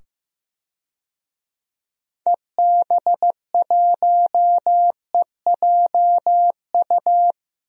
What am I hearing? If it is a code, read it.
Morse code, heard as a keyed tone, about 15 wpm, 700 Hz. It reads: EB1EJU
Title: DXCC-100 Call Signs - 6 Characters - 15wpm